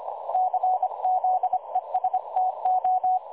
Morse code, heard as a keyed tone, about 25 wpm, 740 Hz. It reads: CZESTO
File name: dlazaawansowanych / nagraniazpasma / websdr recording 3565.0kHz